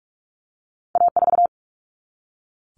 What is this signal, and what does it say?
Morse code, heard as a keyed tone, about 46 wpm, 700 Hz. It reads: A4
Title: Common Cut Numbers - 2-Digit 45wpm